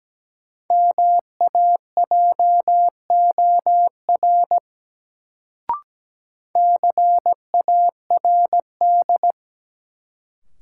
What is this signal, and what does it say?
Morse code, heard as a keyed tone, about 17 wpm, 700 Hz. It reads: MAJOR CARD
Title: Sentences from Top 600 Words - New Word Drill - 17wpm